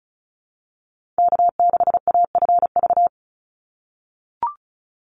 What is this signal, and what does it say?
Morse code, heard as a keyed tone, about 34 wpm, 700 Hz. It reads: K6AFV
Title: USA Call Signs 35wpm